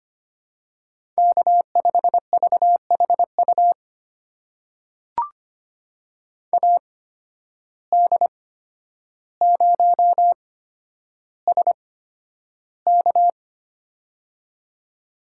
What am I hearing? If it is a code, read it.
Morse code, heard as a keyed tone, about 25 wpm, spaced as for 11 wpm, 700 Hz. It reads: K5VHU AD0SK
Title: USA Call Signs - 5 Characters - 8x Character Spacing - 25wpm